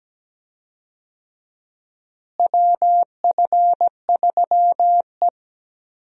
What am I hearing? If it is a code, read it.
Morse code, heard as a keyed tone, about 17 wpm, 700 Hz. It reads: WF3E
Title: USA Call Signs - 4 Characters - Repeated 2x - 17wpm